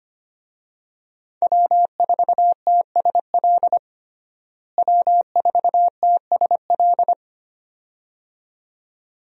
Morse code, heard as a keyed tone, about 25 wpm, 700 Hz. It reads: W4TSL W4TSL
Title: USA Call Signs - 5 Characters - Repeated 2x - 25wpm